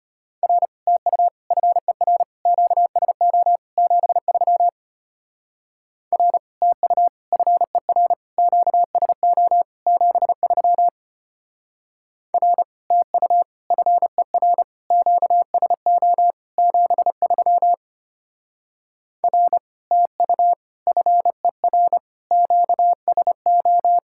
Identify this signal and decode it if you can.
Morse code, heard as a keyed tone, about 30 wpm, 700 Hz. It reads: R TU FER QSO 73 R TU FER QSO 73 R TU FER QSO 73 R TU FER QSO
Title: Speed-Racing Common QSO Phrases Encore - 25wpm